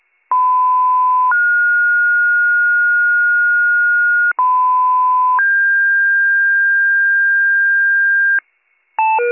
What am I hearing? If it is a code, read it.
Two-tone page, heard as a one-second tone, then a three-second tone, about 1010 then 1430 Hz, about 1010 then 1600 Hz, with noise. It is heard from 0.3 to 8.4 s.